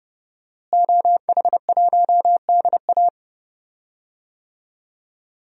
Morse code, heard as a keyed tone, about 30 wpm, 700 Hz. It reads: OH1DA
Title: DXCC-100 Call Signs - 5 Characters - Repeated 2x - 30wpm